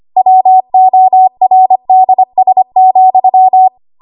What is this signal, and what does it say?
Morse code, heard as a keyed tone, about 25 wpm, 750 Hz. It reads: WORDS,
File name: Archive / 25 WPM / 25WPM